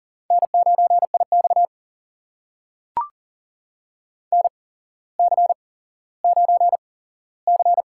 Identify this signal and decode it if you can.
Morse code, heard as a keyed tone, about 42 wpm, spaced as for 17 wpm, 690 Hz. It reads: N9IX NC9C
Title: USA Call Signs - 4 Characters - 8x Character Spacing - 40wpm